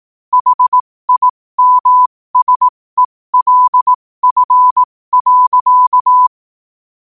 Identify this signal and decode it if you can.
Morse code, heard as a keyed tone, about 18 wpm, spaced as for 15 wpm, 1000 Hz. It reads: HIMSELF.
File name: qotdfiles / 15wpm